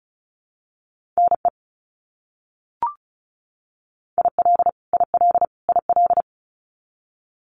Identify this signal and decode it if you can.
Morse code, heard as a keyed tone, about 34 wpm, 700 Hz. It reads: NE IL IL IL
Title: US States Repeated 3x 35wpm